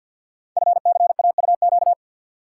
Morse code, heard as a keyed tone, about 52 wpm, 700 Hz. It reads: FCAUQ
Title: Code Groups - 5 Letters - 50wpm